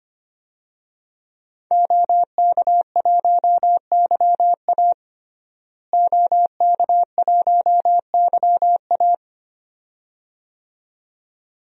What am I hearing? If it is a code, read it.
Morse code, heard as a keyed tone, about 25 wpm, 700 Hz. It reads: OK1YA OK1YA
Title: DXCC-100 Call Signs - 5 Characters - Repeated 2x - 25wpm